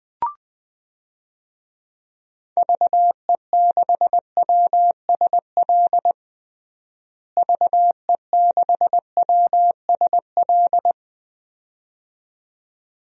Morse code, heard as a keyed tone, about 20 wpm, 700 Hz. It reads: VE6WSL VE6WSL
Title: DXCC-100 Call Signs - 6 Characters - Repeated 2x - 20wpm